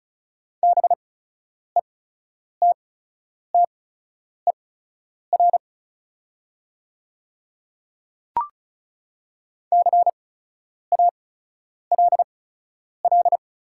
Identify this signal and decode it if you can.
Morse code, heard as a keyed tone, about 36 wpm, spaced as for 10 wpm, 700 Hz. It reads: BETTER CALL
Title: 8x Character Spacing - Top 500 Words - 35wpm